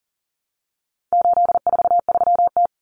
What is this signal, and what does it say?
Morse code, heard as a keyed tone, about 40 wpm, 700 Hz. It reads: 843T
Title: Common Cut Numbers - 4-Digit 40wpm